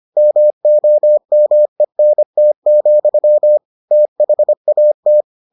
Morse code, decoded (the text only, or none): MOMENT, THAT